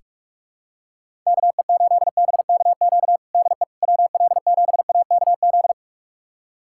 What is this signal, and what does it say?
Morse code, heard as a keyed tone, about 46 wpm, 700 Hz. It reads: KE8BKQ DE WL7AKZ
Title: USA Callsign DE Callsign - 6 Characters - 45wpm